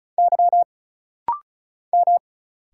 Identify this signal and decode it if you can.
Morse code, heard as a keyed tone, about 35 wpm, 700 Hz. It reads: Y M
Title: Single Letters - Rapid-Fire 35wpm